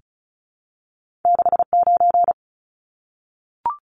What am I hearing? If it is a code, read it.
Morse code, heard as a keyed tone, about 36 wpm, 700 Hz. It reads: 69
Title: Numbers - 2-Digit 35wpm